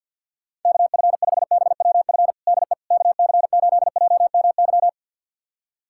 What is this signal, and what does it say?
Morse code, heard as a keyed tone, about 50 wpm, 690 Hz. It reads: KF5BWF DE KC8JMY
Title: USA Callsign DE Callsign - 6 Characters - 50wpm